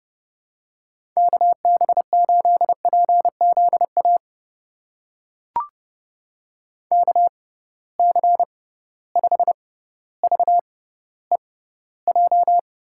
Morse code, heard as a keyed tone, about 30 wpm, spaced as for 15 wpm, 700 Hz. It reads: KB8PZA KC5VEJ
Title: USA Call Signs - 6 Characters - 6x Character Spacing - 30wpm